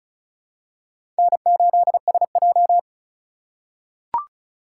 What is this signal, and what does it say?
Morse code, heard as a keyed tone, about 36 wpm, 700 Hz. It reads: N8SJ
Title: USA Call Signs - 3x Character Spacing - 35wpm